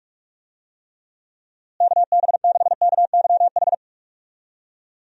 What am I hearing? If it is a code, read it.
Morse code, heard as a keyed tone, about 46 wpm, 700 Hz. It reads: KB6KYH
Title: USA Call Signs - 6 Characters - Repeated 2x - 45wpm